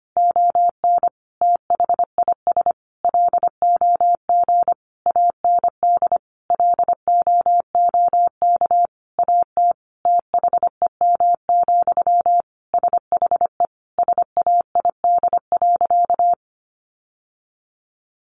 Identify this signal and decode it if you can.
Morse code, heard as a keyed tone, about 25 wpm, 700 Hz. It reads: ON THIS LOG AND LOOK AT THEM, SHE SAID.
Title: Sentences from Top 1400 Words 25wpm